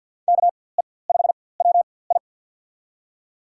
Morse code, heard as a keyed tone, about 48 wpm, spaced as for 23 wpm, 700 Hz. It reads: KE5WI
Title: USA Call Signs - 5 Characters - 4x Character Spacing - 50wpm